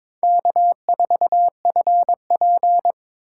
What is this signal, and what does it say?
Morse code, heard as a keyed tone, about 26 wpm, 700 Hz. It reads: K4FP
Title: Speed Racing - USA 1x2 Call Signs 22wpm